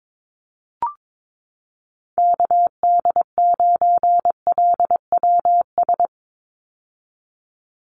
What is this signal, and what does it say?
Morse code, heard as a keyed tone, about 22 wpm, 700 Hz. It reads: KD9LWS
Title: USA Call Signs 22wpm